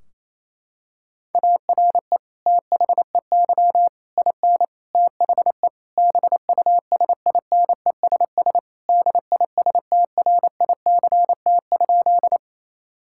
Morse code, heard as a keyed tone, about 27 wpm, 700 Hz. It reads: ARE THEY IN THE BUSINESS DISTRICT?
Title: Sentences from Top 1600 Words 28wpm